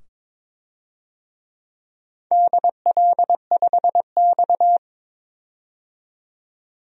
Morse code, heard as a keyed tone, about 22 wpm, 700 Hz. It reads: DL5X
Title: DXCC-100 Call Signs - 4 Characters - 22wpm